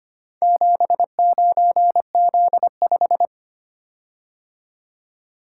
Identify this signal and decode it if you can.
Morse code, heard as a keyed tone, about 25 wpm, 700 Hz. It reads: 79Z5
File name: practice / 25wpm / Four Letter-Number Combinations - 25wpm